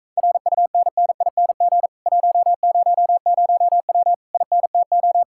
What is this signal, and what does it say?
Morse code, heard as a keyed tone, about 45 wpm, 700 Hz. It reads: RUNNING 100W INTO